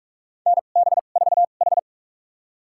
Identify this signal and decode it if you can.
Morse code, heard as a keyed tone, about 44 wpm, spaced as for 32 wpm, 690 Hz. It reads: NB4H
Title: USA Call Signs - 4 Characters - 2x Character Spacing - 45wpm